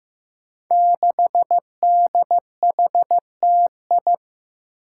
Morse code, heard as a keyed tone, about 15 wpm, 700 Hz. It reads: 6DHTI
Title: Five Challenging Letter-Number Combinations 15wpm